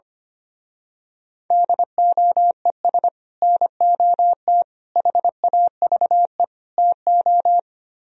Morse code, heard as a keyed tone, about 24 wpm, 700 Hz. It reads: DOES NOT HAVE TO
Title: Speed-Racing - Sets of 4 Words 25wpm